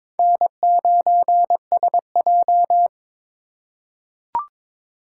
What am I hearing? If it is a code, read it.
Morse code, heard as a keyed tone, about 21 wpm, 700 Hz. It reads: N9SJ
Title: USA Call Signs - 4 Characters - 2x Character Spacing - 22wpm